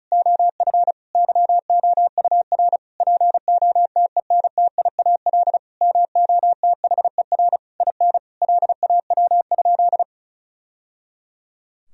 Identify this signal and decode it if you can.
Morse code, heard as a keyed tone, about 34 wpm, 700 Hz. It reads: OF YOUR POTENTIAL MOTHER IN LAW?
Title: Sentences from Top 1000 Words - Review Part 4 - 35wpm